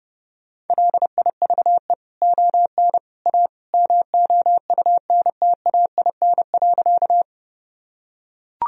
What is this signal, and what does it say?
Morse code, heard as a keyed tone, about 31 wpm, 700 Hz. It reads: LIVE ON A MOUNTAIN.
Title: Sentences from Top 1000 Words - Review Part 1 - 30wpm